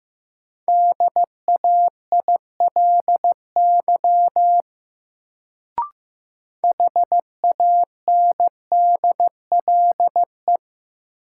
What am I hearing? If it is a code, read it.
Morse code, heard as a keyed tone, about 16 wpm, 700 Hz. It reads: DAILY HANDLE
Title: Sentences from Top 1100 Words - New Word Drill Alt - 15wpm